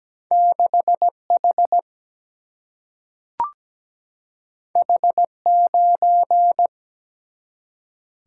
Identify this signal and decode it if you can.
Morse code, heard as a keyed tone, about 17 wpm, 700 Hz. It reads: 6H H9
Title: Single-Letter Single-Number 17wpm